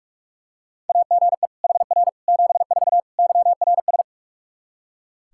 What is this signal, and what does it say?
Morse code, heard as a keyed tone, about 46 wpm, 700 Hz. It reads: AGE HR 74 YRS
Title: Practice QSO - Exchange 3 - More Info - 45wpm